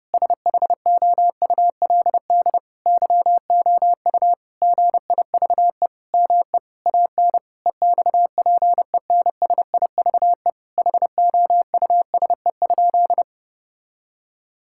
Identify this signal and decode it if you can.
Morse code, heard as a keyed tone, about 30 wpm, 700 Hz. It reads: SHOULD YOU GIVE ME AN EXPENSIVE HOUSE?
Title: Sentences from Top 1000 Words - Review Part 6 - 30wpm